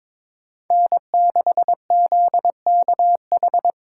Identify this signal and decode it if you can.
Morse code, heard as a keyed tone, about 22 wpm, 700 Hz. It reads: N6ZKH
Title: USA Call Signs - 5 Characters - 2x Character Spacing - 22wpm